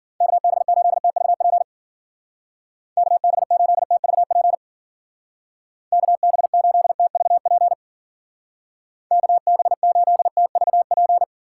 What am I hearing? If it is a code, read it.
Morse code, heard as a keyed tone, about 51 wpm, 700 Hz. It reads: KB8TVP KB8TVP KB8TVP KB8TVP